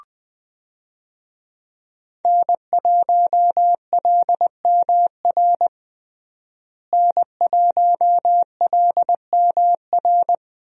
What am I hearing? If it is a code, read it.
Morse code, heard as a keyed tone, about 20 wpm, 700 Hz. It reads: N1LMR N1LMR
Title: USA Call Signs - 5 Characters - Repeated 2x - 20wpm